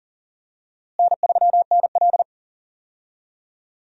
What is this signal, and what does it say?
Morse code, heard as a keyed tone, about 40 wpm, 700 Hz. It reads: N3NL
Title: USA Call Signs - 4 Characters - Repeated 2x - 40wpm